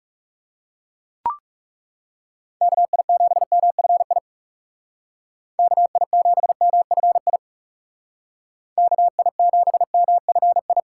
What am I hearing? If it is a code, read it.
Morse code, heard as a keyed tone, about 38 wpm, 700 Hz. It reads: KI7MFI KI7MFI KI7MFI